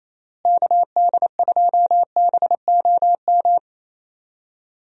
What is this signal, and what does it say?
Morse code, heard as a keyed tone, about 28 wpm, 700 Hz. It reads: KD2BOM
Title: USA Call Signs - 6 Characters - Repeated 3x - 28wpm